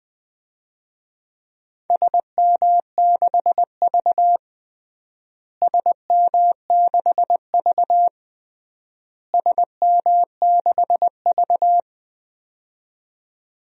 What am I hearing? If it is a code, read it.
Morse code, heard as a keyed tone, about 21 wpm, 700 Hz. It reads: SM6V SM6V SM6V
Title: DXCC-100 Call Signs - 4 Characters - Repeated 3x - 20wpm